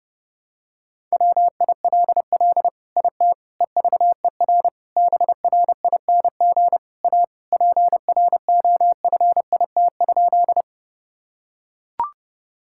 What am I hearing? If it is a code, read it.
Morse code, heard as a keyed tone, about 32 wpm, 700 Hz. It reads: WILL IT EVER BRING A PROFIT?
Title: Sentences from Top 1000 Words - Review Part 5 - 30wpm